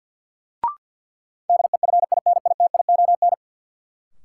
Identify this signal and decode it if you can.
Morse code, heard as a keyed tone, about 51 wpm, 700 Hz. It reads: DEFINITION
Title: Sentences from Top 1000 Words - Review Part 4 - New Word Drill - 50wpm